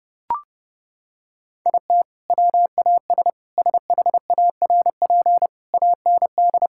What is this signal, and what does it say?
Morse code, heard as a keyed tone, about 32 wpm, 700 Hz. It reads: IT WAS SHARP AND
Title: Sentences from Top 1600 Words 30wpm